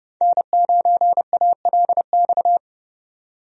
Morse code, heard as a keyed tone, about 30 wpm, 700 Hz. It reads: N9ALX
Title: USA Call Signs - 5 Characters - 30wpm